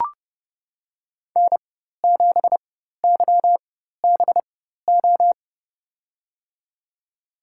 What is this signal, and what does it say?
Morse code, heard as a keyed tone, about 30 wpm, spaced as for 14 wpm, 700 Hz. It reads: N7YBO
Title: USA Call Signs - 5 Characters - 4x Character Spacing - 30wpm